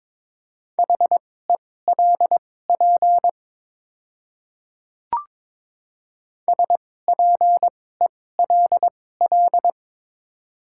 Morse code, heard as a keyed tone, about 22 wpm, spaced as for 16 wpm, 700 Hz. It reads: HELP SPELL
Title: 2x Character Spacing - Top 200 Words - 22wpm